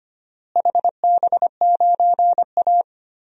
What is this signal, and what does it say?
Morse code, heard as a keyed tone, about 25 wpm, 700 Hz. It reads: HB9A
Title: DXCC-100 Call Signs - 4 Characters - Repeated 2x - 25wpm